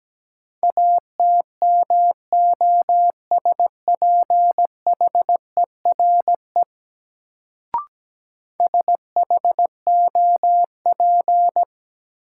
Morse code, heard as a keyed tone, about 17 wpm, 700 Hz. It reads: ATMOSPHERE SHOP
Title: Sentences from Top 1600 Words - New Word Drill Alt - 17wpm